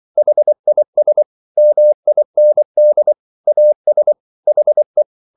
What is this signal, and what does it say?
Morse code, heard as a keyed tone, about 24 wpm, 600 Hz. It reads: HIS MIND AS HE